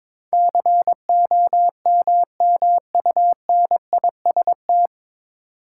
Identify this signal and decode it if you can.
Morse code, heard as a keyed tone, about 22 wpm, 700 Hz. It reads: COMMUNIST
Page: Morse Code Ninja practice 22wpm